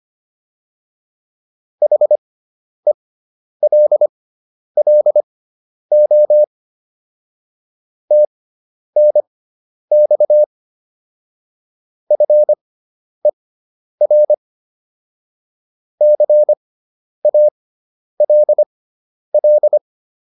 Morse code, heard as a keyed tone, about 25 wpm, spaced as for 10 wpm, 600 Hz. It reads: HELLO TNX FER CALL
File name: QSO206 10